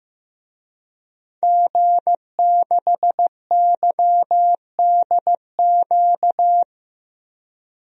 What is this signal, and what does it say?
Morse code, heard as a keyed tone, about 15 wpm, 700 Hz. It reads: G6YDQ
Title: DXCC-100 Call Signs - 5 Characters - Repeated 2x - 15wpm